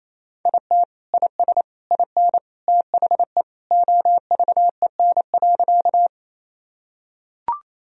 Morse code, heard as a keyed tone, about 28 wpm, 700 Hz. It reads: IT IS IN THE OVEN.
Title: Sentences from Top 1000 Words - Review Part 4 - 28wpm